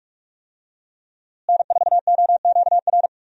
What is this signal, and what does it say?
Morse code, heard as a keyed tone, about 47 wpm, 700 Hz. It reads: N4OQR
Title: USA Call Signs - 5 Characters - Repeated 2x - 45wpm